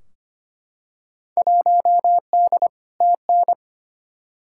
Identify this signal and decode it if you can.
Morse code, heard as a keyed tone, about 25 wpm, 700 Hz. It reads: 1D TN
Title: Field Day Exchanges - 25wpm